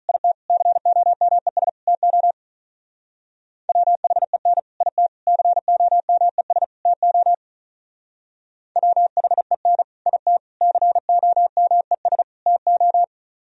Morse code, heard as a keyed tone, about 42 wpm, 700 Hz. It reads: IT COMES TO WHEN IT COMES TO WHEN IT COMES TO
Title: Speed-Racing - Sets of 4 Words 35wpm